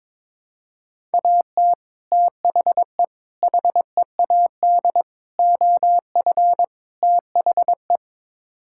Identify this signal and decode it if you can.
Morse code, heard as a keyed tone, about 22 wpm, 700 Hz. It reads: AT THE HEAD OF THE
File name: practice / 22wpm / Build A Phrase - 5-Word Phrases - 22wpm